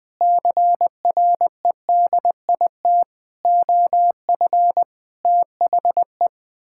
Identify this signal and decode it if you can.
Morse code, heard as a keyed tone, about 20 wpm, 700 Hz. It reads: CREDIT OF THE
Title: Sentences with 9-Word Phrases - Spoken Beginning - 20wpm